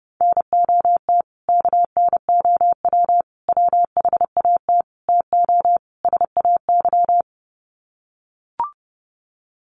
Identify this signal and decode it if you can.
Morse code, heard as a keyed tone, about 30 wpm, 700 Hz. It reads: NOT KNOW WHAT TO SAY
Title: Sets of 5 Words 30wpm